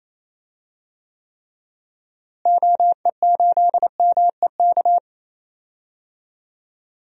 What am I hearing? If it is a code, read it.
Morse code, heard as a keyed tone, about 28 wpm, 700 Hz. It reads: OE8MEK
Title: DXCC-100 Call Signs - 6 Characters - Repeated 2x - 28wpm